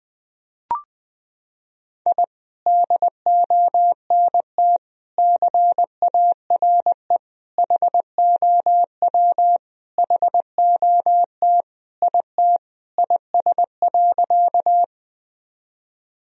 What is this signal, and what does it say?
Morse code, heard as a keyed tone, about 20 wpm, 700 Hz. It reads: I DONT CARE HOW HOT IT IS.
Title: Sentences from Top 1000 Words - Review Part 3 - 20wpm